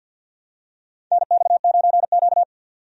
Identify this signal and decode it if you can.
Morse code, heard as a keyed tone, about 51 wpm, 700 Hz. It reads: NX9Q
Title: USA Call Signs - 4 Characters - 50wpm